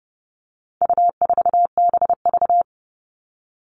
Morse code, heard as a keyed tone, about 30 wpm, 700 Hz. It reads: U4BV
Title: All Cut Numbers - 4-Digit 30wpm